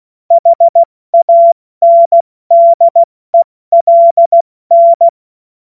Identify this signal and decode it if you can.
Morse code, heard as a keyed tone, about 16 wpm, 680 Hz. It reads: HANDELN
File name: Morsekurs / Witze / Witz 01. Juni 2021, 70 F80